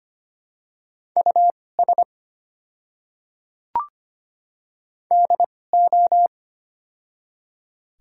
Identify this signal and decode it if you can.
Morse code, heard as a keyed tone, about 25 wpm, spaced as for 18 wpm, 700 Hz. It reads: US DO
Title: 2x Character Spacing - Top 300 Words - 25wpm